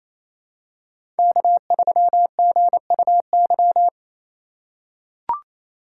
Morse code, heard as a keyed tone, about 28 wpm, 700 Hz. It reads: K3GUY
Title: USA Call Signs - 5 Characters - 2x Character Spacing - 28wpm